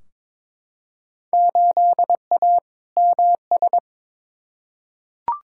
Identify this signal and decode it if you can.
Morse code, heard as a keyed tone, about 22 wpm, 700 Hz. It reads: 8A MS
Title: Field Day Exchanges - 22wpm